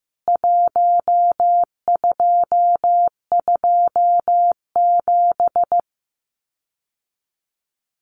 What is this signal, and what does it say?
Morse code, heard as a keyed tone, about 15 wpm, 700 Hz. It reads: 1227
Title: CWT Exchange 15wpm